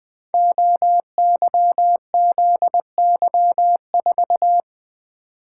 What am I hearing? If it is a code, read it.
Morse code, heard as a keyed tone, about 20 wpm, 700 Hz. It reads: OYZY4
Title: Code Groups - 5 Letter:Numbers - 20wpm